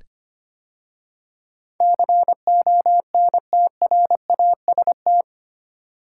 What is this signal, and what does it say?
Morse code, heard as a keyed tone, about 25 wpm, 700 Hz. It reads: CONTRAST
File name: practice / 25wpm / 8 Letter Words - First 3 Letters Given - 25wpm